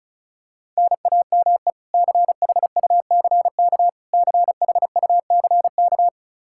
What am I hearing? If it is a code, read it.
Morse code, heard as a keyed tone, about 36 wpm, 700 Hz. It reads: NAME CHUCK CHUCK
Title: K1USN SST QSO Practice 35wpm